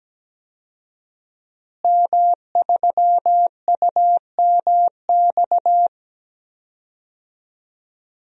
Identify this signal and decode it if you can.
Morse code, heard as a keyed tone, about 17 wpm, 700 Hz. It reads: M3UMX
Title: DXCC-100 Call Signs - 5 Characters - Repeated 3x - 17wpm